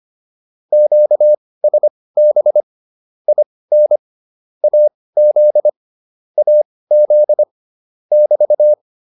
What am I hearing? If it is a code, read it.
Morse code, heard as a keyed tone, about 25 wpm, spaced as for 18 wpm, 600 Hz. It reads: QSB IN AZ AZ <BT>